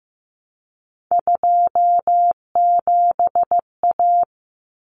Morse code, heard as a keyed tone, about 15 wpm, 700 Hz. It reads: 27A
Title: Common Cut Numbers - 3-Digit 15wpm